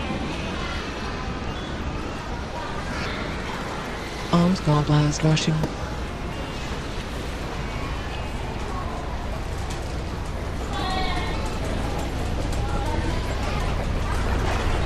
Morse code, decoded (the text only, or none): none